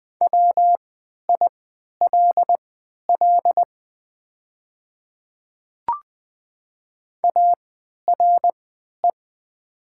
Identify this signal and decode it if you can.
Morse code, heard as a keyed tone, about 20 wpm, spaced as for 11 wpm, 700 Hz. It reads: WILL ARE
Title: 3x Character Spacing - Top 100 Words - 20wpm